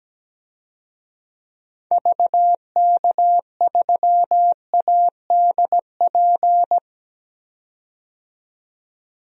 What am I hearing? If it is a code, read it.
Morse code, heard as a keyed tone, about 17 wpm, 700 Hz. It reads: VK3ADP
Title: DXCC-100 Call Signs - 6 Characters - 17wpm